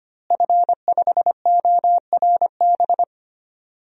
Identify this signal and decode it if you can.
Morse code, heard as a keyed tone, about 25 wpm, 700 Hz. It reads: F5ORB
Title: DXCC-100 Call Signs - 5 Characters - Repeated 2x - 25wpm